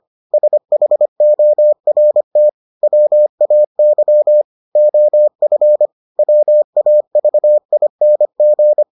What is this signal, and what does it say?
Morse code, heard as a keyed tone, about 25 wpm, 600 Hz. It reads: SHORT WAY OF WAVING